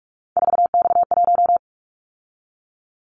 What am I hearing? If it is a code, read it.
Morse code, heard as a keyed tone, about 44 wpm, 700 Hz. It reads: 4X1
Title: Callsign Prefixes for Top 50 DXCC - 45wpm